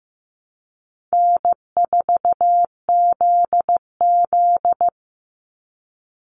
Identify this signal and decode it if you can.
Morse code, heard as a keyed tone, about 15 wpm, 700 Hz. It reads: N4ZZ
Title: Call Signs Repeated 3x - 15wpm